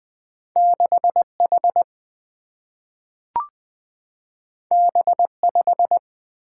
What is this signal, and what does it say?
Morse code, heard as a keyed tone, about 20 wpm, 700 Hz. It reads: 6H B5
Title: Single-Letter Single-Number 20wpm